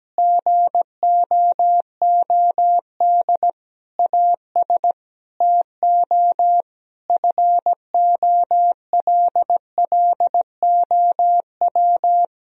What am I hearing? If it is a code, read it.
Morse code, heard as a keyed tone, about 17 wpm, 700 Hz. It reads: GOOD AS TO FOLLOW
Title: Sentences from Top 1000 Words - Review Part 1 - 17wpm